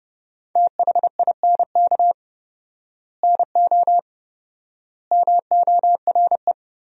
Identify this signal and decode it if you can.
Morse code, heard as a keyed tone, about 30 wpm, 700 Hz. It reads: THINK NO MORE
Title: 4x Word-Spacing - Sentences from Top 200 Words - 30wpm